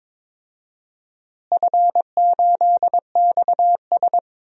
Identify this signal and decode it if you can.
Morse code, heard as a keyed tone, about 22 wpm, 700 Hz. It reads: F8XS
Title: DXCC-100 Call Signs - 4 Characters - Repeated 3x - 22wpm